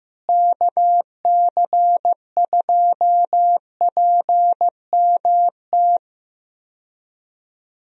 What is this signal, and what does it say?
Morse code, heard as a keyed tone, about 16 wpm, 700 Hz. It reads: KC2PMT
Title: USA Call Signs - 6 Characters - Repeated 2x - 15wpm